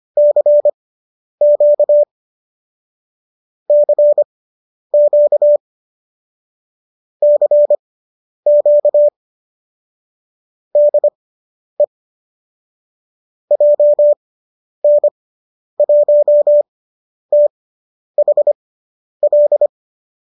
Morse code, heard as a keyed tone, about 25 wpm, spaced as for 10 wpm, 590 Hz. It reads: CQ CQ CQ DE JN1THL